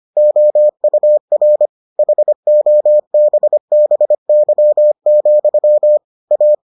Morse code, heard as a keyed tone, about 25 wpm, 600 Hz. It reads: OUR HOBBY, A